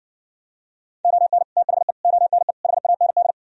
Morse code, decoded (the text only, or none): ON THE ONE HAND